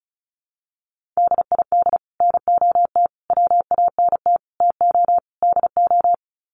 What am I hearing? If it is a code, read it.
Morse code, heard as a keyed tone, about 34 wpm, 700 Hz. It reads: DID NOT WANT TO DO
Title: Sets of 5 Words Encore 35wpm